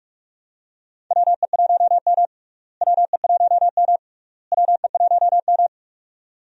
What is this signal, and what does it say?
Morse code, heard as a keyed tone, about 46 wpm, 700 Hz. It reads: WE1M WE1M WE1M
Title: USA Call Signs - 4 Characters - Repeated 3x - 45wpm